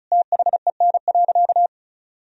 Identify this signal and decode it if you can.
Morse code, heard as a keyed tone, about 36 wpm, 700 Hz. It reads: THEN.
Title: Sentences from Top 200 Words and Abbreviations 35wpm